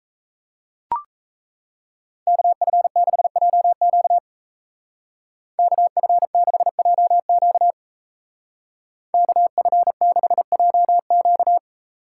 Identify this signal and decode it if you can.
Morse code, heard as a keyed tone, about 34 wpm, 700 Hz. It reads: KF6JQ KF6JQ KF6JQ
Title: Speed Racing - USA 2x2 Call Signs 28wpm